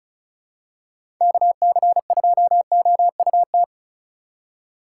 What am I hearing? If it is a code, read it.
Morse code, heard as a keyed tone, about 36 wpm, 700 Hz. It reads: KC2OUT
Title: USA Call Signs - 8x Character Spacing - 35wpm